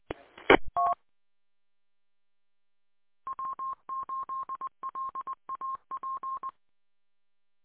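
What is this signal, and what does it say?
Morse code, heard as a keyed tone, about 20 wpm, 1060 Hz. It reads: ET W8LAP
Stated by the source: Morse code recorded from W8LAP 146.620 repeater audio archive.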